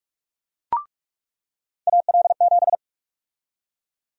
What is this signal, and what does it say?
Morse code, heard as a keyed tone, about 46 wpm, 700 Hz. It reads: AL7
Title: USA Call Sign Prefixes - 45wpm